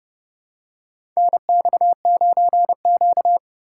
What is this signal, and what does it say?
Morse code, heard as a keyed tone, about 30 wpm, 700 Hz. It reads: NX9Q
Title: USA Call Signs - 4x Character Spacing - 30wpm